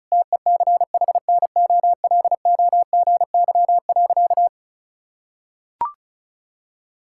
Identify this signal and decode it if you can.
Morse code, heard as a keyed tone, about 36 wpm, 700 Hz. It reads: TECHNOLOGY.
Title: Sentences from Top 1000 Words - Review Part 5 - 35wpm